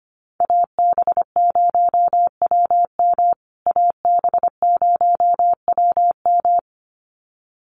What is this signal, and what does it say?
Morse code, heard as a keyed tone, about 25 wpm, 700 Hz. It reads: AB0WM AB0WM
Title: USA Call Signs Repeated 2x 25wpm